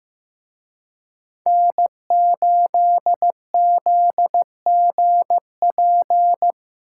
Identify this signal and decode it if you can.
Morse code, heard as a keyed tone, about 18 wpm, 700 Hz. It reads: N8ZGP
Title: Speed Racing - USA 1x3 Call Signs 15wpm